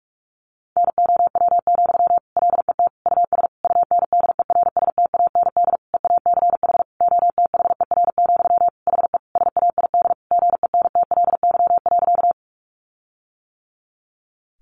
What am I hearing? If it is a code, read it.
Morse code, heard as a keyed tone, about 44 wpm, 700 Hz. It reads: NOW, LET US UNDERSTAND EACH OTHER, HE SAID GENTLY.
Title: Sentences from Top 1500 Words 45wpm